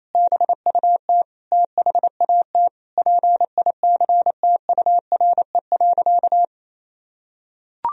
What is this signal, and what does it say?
Morse code, heard as a keyed tone, about 28 wpm, 700 Hz. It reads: BUT THAT PICTURE.
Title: Sentences from Top 1000 Words - Review Part 1 - 28wpm